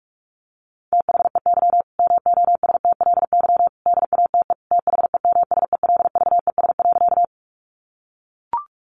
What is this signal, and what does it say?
Morse code, heard as a keyed tone, about 44 wpm, 700 Hz. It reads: THEY MOSTLY DATE THEMSELVES.
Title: Sentences from Top 1300 Words 45wpm